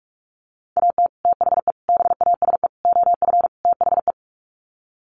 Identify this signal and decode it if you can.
Morse code, heard as a keyed tone, about 44 wpm, 700 Hz. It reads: AT THE BASE OF THE
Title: Sets of 5 Words 45wpm